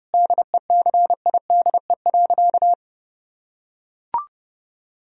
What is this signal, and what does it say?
Morse code, heard as a keyed tone, about 32 wpm, 700 Hz. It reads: DECIDE.
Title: Sentences from Top 1000 Words - Review Part 3 - 30wpm